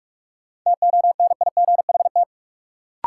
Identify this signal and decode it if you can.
Morse code, heard as a keyed tone, about 46 wpm, 700 Hz. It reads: TONIGHT
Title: Sentences from Top 600 Words - New Word Drill - 45wpm